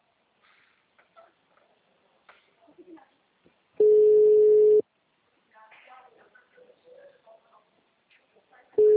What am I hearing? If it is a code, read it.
Ringback tone (Europe, 425 Hz), with speech.